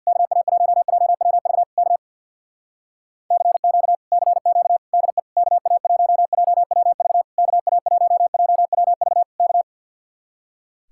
Morse code, heard as a keyed tone, about 64 wpm, 700 Hz. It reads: KA1JWV K CQ CQ DE KA1JWV KA1JWV K